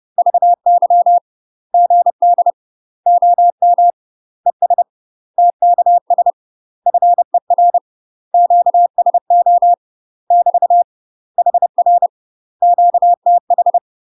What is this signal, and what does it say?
Morse code, heard as a keyed tone, about 30 wpm, 700 Hz. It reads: VY GD OM ES TKS FER QSO <BT> HR QTH
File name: UK MASTER QSOs TRAINING / 30WPM / UK MASTER QSOs 0051